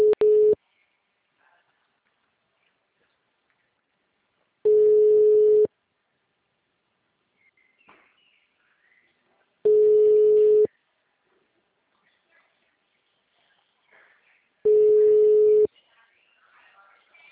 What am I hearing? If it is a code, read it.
Ringback tone (Europe, 425 Hz), with speech.